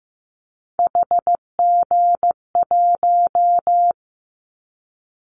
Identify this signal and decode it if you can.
Morse code, heard as a keyed tone, about 15 wpm, 700 Hz. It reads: HG1
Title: Callsign Prefixes for Top 25 DXCC - Repeated 3x - 15wpm